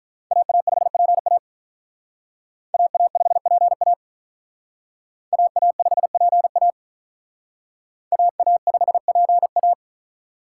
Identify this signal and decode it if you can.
Morse code, heard as a keyed tone, about 45 wpm, 700 Hz. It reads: AA5PA AA5PA AA5PA AA5PA